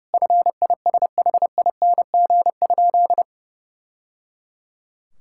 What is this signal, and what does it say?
Morse code, heard as a keyed tone, about 28 wpm, 700 Hz. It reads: FISHING?
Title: Sentences from Top 1000 Words - Review Part 5 - 30wpm